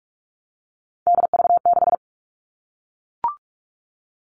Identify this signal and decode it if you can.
Morse code, heard as a keyed tone, about 44 wpm, 700 Hz. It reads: DV6
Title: All Cut Numbers - 3-Digit 45wpm